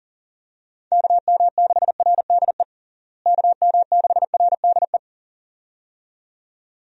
Morse code, heard as a keyed tone, about 37 wpm, 700 Hz. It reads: KM6RDE KM6RDE
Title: USA Call Signs - 6 Characters - Repeated 2x - 40wpm